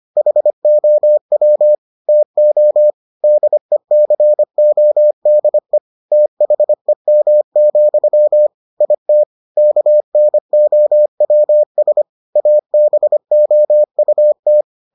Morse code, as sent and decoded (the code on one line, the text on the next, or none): .... --- .-- / - --- / -.. -.-. --- -.. / - .... -- --..-- / .. - / -.- -. --- .-- ... / .- -... --- ..- -
HOW TO DECODE THEM, IT KNOWS ABOUT